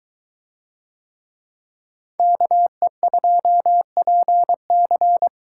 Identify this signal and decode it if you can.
Morse code, heard as a keyed tone, about 21 wpm, 700 Hz. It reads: KE2PC